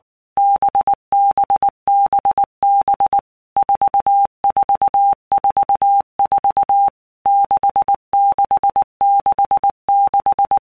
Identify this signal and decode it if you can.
Morse code, heard as a keyed tone, about 19 wpm, 800 Hz. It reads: BBBB 4444 6666
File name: Basic Letter Groups By Blocks / Groups@18 / VB46@18